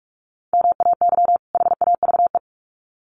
Morse code, heard as a keyed tone, about 44 wpm, 700 Hz. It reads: MAY HAVE
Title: Sets of 2 Words 45wpm